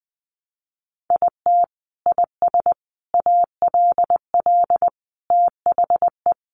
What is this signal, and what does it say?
Morse code, heard as a keyed tone, about 20 wpm, 700 Hz. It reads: IT IS ALL THE
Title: Sentences from Top 200 Words 20wpm